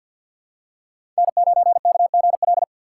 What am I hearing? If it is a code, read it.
Morse code, heard as a keyed tone, about 46 wpm, 700 Hz. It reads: N9KGL
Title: USA Call Signs - 5 Characters - 3x Character Spacing - 50wpm